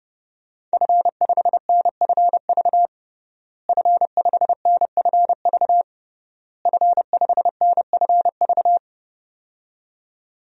Morse code, heard as a keyed tone, about 30 wpm, 700 Hz. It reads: F5NFV F5NFV F5NFV
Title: DXCC-100 Call Signs - 5 Characters - Repeated 3x - 30wpm